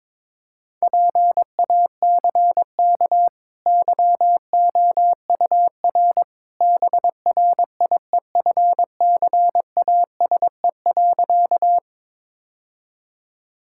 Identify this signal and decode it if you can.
Morse code, heard as a keyed tone, about 22 wpm, 690 Hz. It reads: PACK YOUR BRIEFCASE.
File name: practice / 22wpm / Easy Sentences with Hints - 22wpm